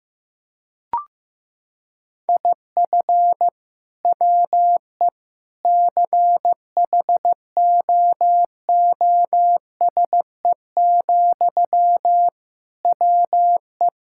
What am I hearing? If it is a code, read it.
Morse code, heard as a keyed tone, about 15 wpm, 700 Hz. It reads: IF WE CHOOSE, WE
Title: Sentences from Top 1000 Words - Review Part 6 - 15wpm